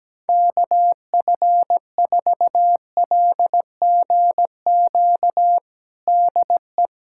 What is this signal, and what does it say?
Morse code, heard as a keyed tone, about 17 wpm, 700 Hz. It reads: KF4LGQ DE
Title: USA Callsign DE Callsign - 6 Characters - 17wpm